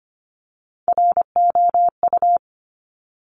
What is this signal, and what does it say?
Morse code, heard as a keyed tone, about 25 wpm, 700 Hz. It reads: ROU
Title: Top Three Letter Combinations in Words 25wpm